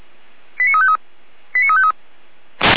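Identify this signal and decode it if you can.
Five-tone selective call (ZVEI), heard as quick runs of five tones, tones of about 70 ms. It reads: 89353 89353
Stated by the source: ZVEI five-tone call 89164 received on SDR, Feuerwehr Orlamünde.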